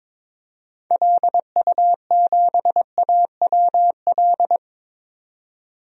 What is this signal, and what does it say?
Morse code, heard as a keyed tone, about 22 wpm, 700 Hz. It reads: LU7AWL
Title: DXCC-100 Call Signs - 6 Characters - 22wpm